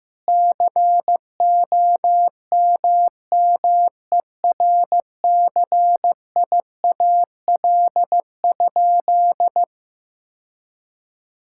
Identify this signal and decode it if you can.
Morse code, heard as a keyed tone, about 15 wpm, 700 Hz. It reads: COMMERCIAL?
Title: Sentences from Top 1000 Words - Review Part 5 - 15wpm